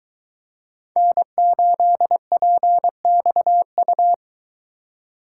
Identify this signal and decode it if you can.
Morse code, heard as a keyed tone, about 21 wpm, 700 Hz. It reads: N8PXU